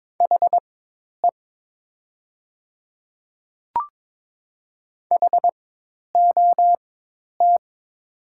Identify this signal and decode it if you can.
Morse code, heard as a keyed tone, about 22 wpm, spaced as for 10 wpm, 700 Hz. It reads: HE HOT
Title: 4x Character Spacing - Top 100 Words - 22wpm